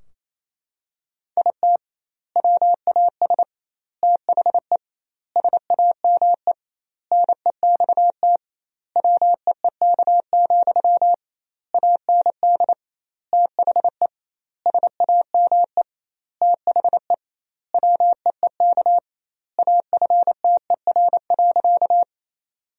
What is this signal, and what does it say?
Morse code, heard as a keyed tone, about 28 wpm, 700 Hz. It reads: IT WAS THE SAME NEXT WEEK, AND THE SAME THE WEEK AFTER.
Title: 2x Word-Spacing - Sentences from Top 500 Words - 28wpm